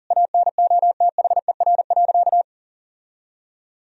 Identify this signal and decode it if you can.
Morse code, heard as a keyed tone, about 40 wpm, 700 Hz. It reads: ANOTHER.
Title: Sentences from Top 1000 Words - Review Part 2 - 40wpm